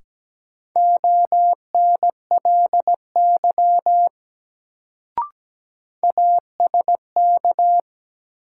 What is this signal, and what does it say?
Morse code, heard as a keyed tone, about 17 wpm, 700 Hz. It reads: ONLY ASK
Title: Sentences from Top 200 Words - New Word Drill - 17wpm